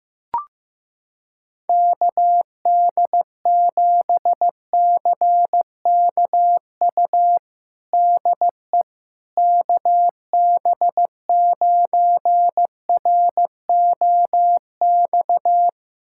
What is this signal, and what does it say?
Morse code, heard as a keyed tone, about 15 wpm, 700 Hz. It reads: KD7CKU DE KB9ROX
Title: USA Callsign DE Callsign 15wpm